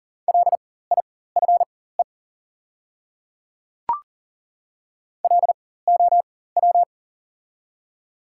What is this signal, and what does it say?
Morse code, heard as a keyed tone, about 40 wpm, spaced as for 19 wpm, 700 Hz. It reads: LIFE LOW
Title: 4x Character Spacing - Top 300 Words - 40wpm